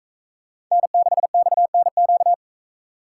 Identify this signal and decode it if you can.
Morse code, heard as a keyed tone, about 45 wpm, 700 Hz. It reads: N6XNQ